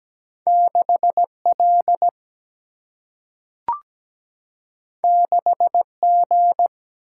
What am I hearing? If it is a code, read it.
Morse code, heard as a keyed tone, about 17 wpm, 700 Hz. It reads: 6L 6G
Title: Single-Letter Single-Number 17wpm